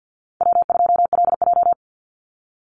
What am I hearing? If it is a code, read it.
Morse code, heard as a keyed tone, about 50 wpm, 700 Hz. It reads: W2LJ
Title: Easy Call Signs - 50wpm